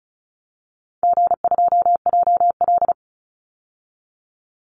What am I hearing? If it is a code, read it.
Morse code, heard as a keyed tone, about 34 wpm, 700 Hz. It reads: G2JL
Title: Easy Call Signs Repeated 3x - 35wpm